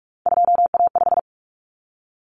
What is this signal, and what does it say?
Morse code, heard as a keyed tone, about 44 wpm, 700 Hz. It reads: 2A5
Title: Common Cut Numbers - 3-Digit 45wpm